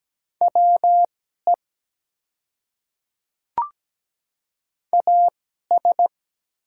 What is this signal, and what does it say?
Morse code, heard as a keyed tone, about 17 wpm, spaced as for 12 wpm, 700 Hz. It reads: WE AS
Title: 2x Character Spacing - Top 100 Words - 17wpm